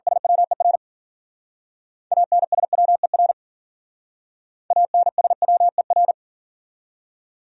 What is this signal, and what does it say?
Morse code, heard as a keyed tone, about 47 wpm, 700 Hz. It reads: SWER ANSWER ANSWER
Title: Speed-Racing Top 500 Words 40wpm